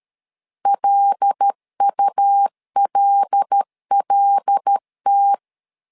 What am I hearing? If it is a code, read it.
Morse code, heard as a keyed tone, about 13 wpm, 800 Hz. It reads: LULLT
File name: LEZIONI / 5 LBVC / 5 L 12wpm 1913